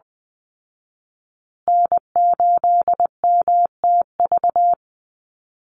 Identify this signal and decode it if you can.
Morse code, heard as a keyed tone, about 20 wpm, 700 Hz. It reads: N8MTV
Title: Call Signs Repeated 3x - 20wpm